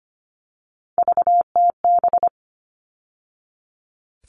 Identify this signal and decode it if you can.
Morse code, heard as a keyed tone, about 25 wpm, 700 Hz. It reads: VTB
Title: All Cut Numbers - 3-Digit 25wpm